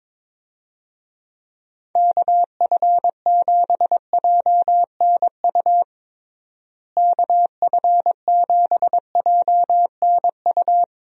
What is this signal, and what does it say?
Morse code, heard as a keyed tone, about 22 wpm, 700 Hz. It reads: KF7JNU KF7JNU
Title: USA Call Signs - 6 Characters - Repeated 2x - 22wpm